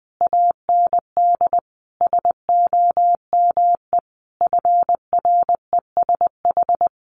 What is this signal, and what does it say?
Morse code, heard as a keyed tone, about 20 wpm, 700 Hz. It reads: AND SOME FRESH
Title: Sentences from Top 1500 Words 20wpm